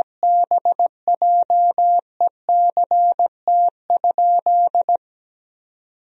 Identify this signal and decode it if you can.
Morse code, heard as a keyed tone, about 17 wpm, 700 Hz. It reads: BJECT?
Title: Sentences from Top 1000 Words - Review Part 3 - 17wpm